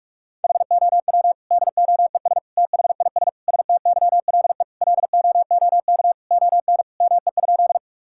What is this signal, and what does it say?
Morse code, heard as a keyed tone, about 45 wpm, 700 Hz. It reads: HOW DOES THIS STYLE LOOK ON ME?
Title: Sentences from Top 1000 Words - Review Part 5 - 45wpm